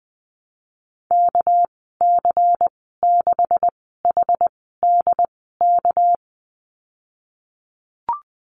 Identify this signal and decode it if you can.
Morse code, heard as a keyed tone, about 20 wpm, spaced as for 15 wpm, 700 Hz. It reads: KC6HDK
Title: 200 Lesson 32g - Letter k - Call Signs 20wpm